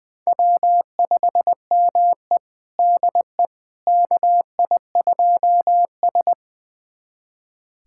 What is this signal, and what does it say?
Morse code, heard as a keyed tone, about 20 wpm, 700 Hz. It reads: W5ME DE KI2S
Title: USA Callsign DE Callsign - 4 Characters - 20wpm